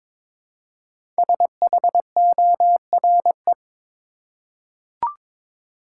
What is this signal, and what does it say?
Morse code, heard as a keyed tone, about 22 wpm, 700 Hz. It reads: SHORE
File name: 5 Letter Words - First 3 Letters Given - 22wpm